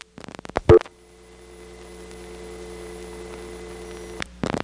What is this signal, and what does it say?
Dial tone (North America), 350 + 440 Hz, with noise.
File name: GroupBell / MariettaMap-unfinished / Marietta1